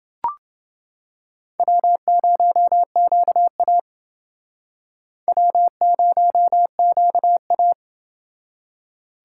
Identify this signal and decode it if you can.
Morse code, heard as a keyed tone, about 29 wpm, 700 Hz. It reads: W0QA W0QA